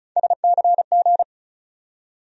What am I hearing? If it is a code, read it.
Morse code, heard as a keyed tone, about 34 wpm, 700 Hz. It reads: SCG
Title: USA Call Sign Postfixes - 3 Letters 35wpm